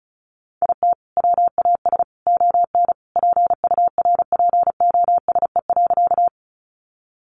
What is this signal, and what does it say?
Morse code, heard as a keyed tone, about 34 wpm, 700 Hz. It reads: IT WAS ON PURPOSE.
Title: Sentences from Top 800 Words 35wpm